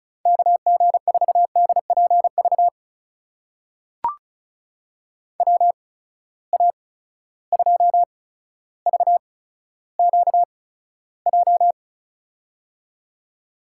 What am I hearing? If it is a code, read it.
Morse code, heard as a keyed tone, about 36 wpm, spaced as for 15 wpm, 700 Hz. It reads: KG4DPV WA2VQJ
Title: USA Call Signs - 6 Characters - 8x Character Spacing - 35wpm